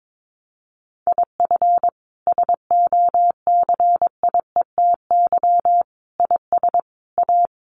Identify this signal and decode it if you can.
Morse code, heard as a keyed tone, about 22 wpm, 700 Hz. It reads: IF SOCIETY IS A
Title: Sentences from Top 1500 Words 22wpm